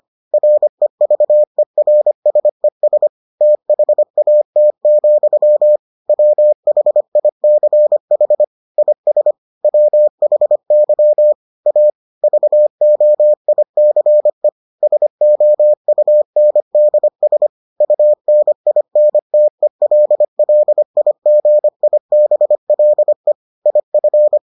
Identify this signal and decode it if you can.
Morse code, heard as a keyed tone, about 25 wpm, 600 Hz. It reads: REVERSES THAT, WHICH IS WHY A VOICE SOUNDS UNINTELLIGIBLE IF